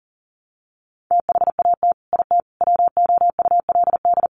Morse code, heard as a keyed tone, about 40 wpm, 700 Hz. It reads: THAT IT WOULD
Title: Sets of 4 Words 40wpm